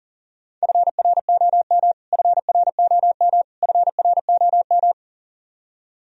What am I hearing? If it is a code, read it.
Morse code, heard as a keyed tone, about 40 wpm, 700 Hz. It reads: FROM FROM FROM
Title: Top 100 Words Repeated 40wpm